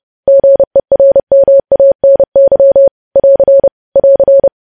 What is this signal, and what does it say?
Morse code, heard as a keyed tone, about 30 wpm, 550 Hz. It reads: GERMANY <AR> <AR>
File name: podcast / News headlines-1x-30x30